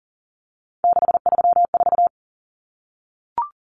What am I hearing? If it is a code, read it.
Morse code, heard as a keyed tone, about 40 wpm, 700 Hz. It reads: 634
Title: Common Cut Numbers - 3-Digit 40wpm